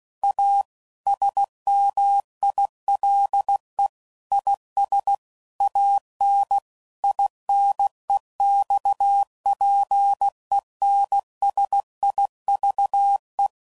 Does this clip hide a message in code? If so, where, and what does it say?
Yes: Morse code from 0.2 to 13.5 s, heard as a keyed tone, about 16 wpm, 800 Hz. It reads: A SMILE IS AN INEXPENSIVE